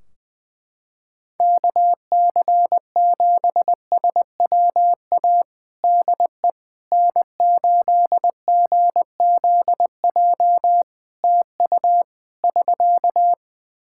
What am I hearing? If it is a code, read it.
Morse code, heard as a keyed tone, about 20 wpm, 700 Hz. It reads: KC7SWA DE N8GZJ TU <SK>
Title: Practice QSO - Exchange 4 - Ending - 20wpm